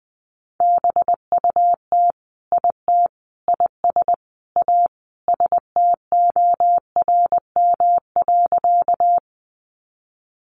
Morse code, heard as a keyed tone, about 20 wpm, 700 Hz. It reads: BUT IT IS A STORM.
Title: Sentences from Top 1500 Words 20wpm